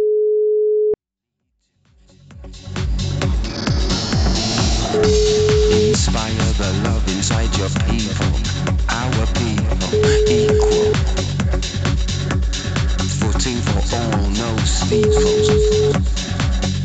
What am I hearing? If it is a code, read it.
Ringback tone (Europe, 425 Hz), with speech.